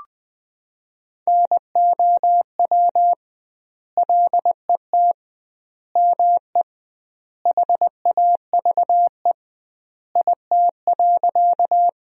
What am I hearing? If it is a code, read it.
Morse code, heard as a keyed tone, about 20 wpm, 700 Hz. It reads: NOW LET ME HAVE IT.
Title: 2x Word-Spacing - Sentences from Top 300 Words - 20wpm